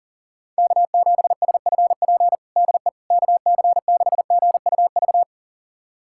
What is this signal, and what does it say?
Morse code, heard as a keyed tone, about 40 wpm, 700 Hz. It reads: K7SFP DE KC6GUV
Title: USA Callsign DE Callsign 40wpm